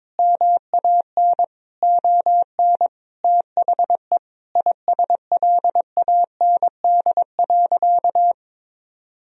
Morse code, heard as a keyed tone, about 22 wpm, 700 Hz. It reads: MAN ON THE ISLAND.
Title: Sentences from Top 1000 Words - Review Part 3 - 22wpm